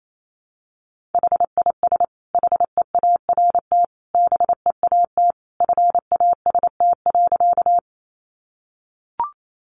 Morse code, heard as a keyed tone, about 28 wpm, 700 Hz. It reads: HIS HEART BEAT FAST.
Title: Sentences from Top 1300 Words 28wpm